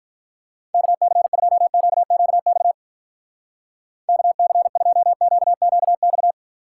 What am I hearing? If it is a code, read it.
Morse code, heard as a keyed tone, about 43 wpm, 700 Hz. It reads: KC2QQX KC2QQX
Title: Speed Racing - USA 2x3 Call Signs 35wpm